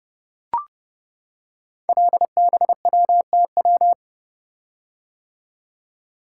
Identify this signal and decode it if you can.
Morse code, heard as a keyed tone, about 30 wpm, 700 Hz. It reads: LBWTW
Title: Code Groups - 5 Letter:Numbers - 30wpm